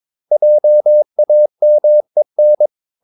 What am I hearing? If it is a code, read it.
Morse code, heard as a keyed tone, about 22 wpm, 600 Hz. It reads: JAMEN
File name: Ferie QRS 22wpm